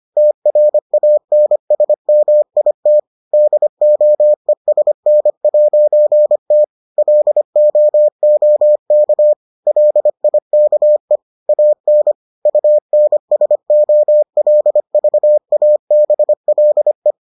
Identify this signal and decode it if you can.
Morse code, heard as a keyed tone, about 25 wpm, 600 Hz. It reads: TRANSMIT DOESN'T LOOK LIKE AN UNSOLVABLE